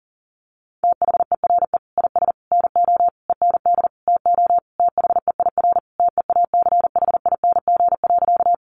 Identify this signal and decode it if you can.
Morse code, heard as a keyed tone, about 40 wpm, 700 Hz. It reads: THERE IS NO END TO THEIR TEACHING.
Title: Sentences from Top 1100 Words 40wpm